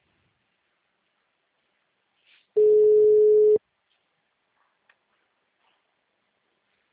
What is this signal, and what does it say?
Ringback tone (Europe, 425 Hz).